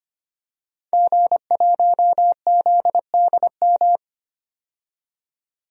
Morse code, heard as a keyed tone, about 25 wpm, 700 Hz. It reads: G1ZDM
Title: DXCC-100 Call Signs - 5 Characters - 25wpm